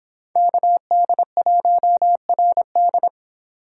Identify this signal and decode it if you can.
Morse code, heard as a keyed tone, about 29 wpm, 700 Hz. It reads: KD1RD